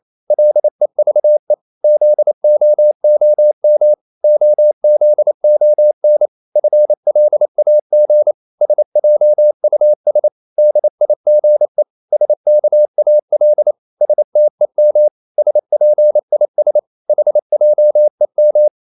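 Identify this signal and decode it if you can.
Morse code, heard as a keyed tone, about 28 wpm, 600 Hz. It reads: LEVE ZOOM OZON FLAG SJUS DIGE SKAL STEM SPIS HJEM
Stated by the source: Danish short words sent in Morse at 28 wpm, from CW med Gnister.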